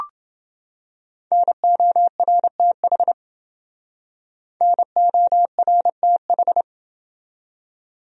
Morse code, heard as a keyed tone, about 29 wpm, 700 Hz. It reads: NORTH NORTH